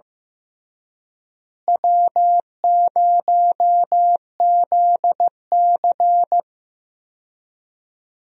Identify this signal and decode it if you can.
Morse code, heard as a keyed tone, about 15 wpm, 700 Hz. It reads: W0ZC
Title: USA Call Signs - 4 Characters - Repeated 3x - 15wpm